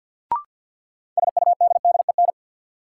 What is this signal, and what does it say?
Morse code, heard as a keyed tone, about 51 wpm, 700 Hz. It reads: SUDDEN
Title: Sets of 4 Words - New Word Drill - 50wpm